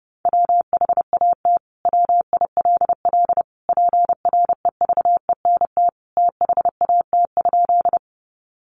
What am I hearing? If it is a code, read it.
Morse code, heard as a keyed tone, about 28 wpm, 700 Hz. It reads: WHAT WILL PREVENT THAT?
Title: Sentences from Top 1100 Words 30wpm